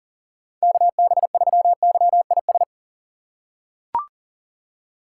Morse code, heard as a keyed tone, about 40 wpm, 700 Hz. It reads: KB3YIS
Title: USA Call Signs - 6 Characters - 6x Character Spacing - 40wpm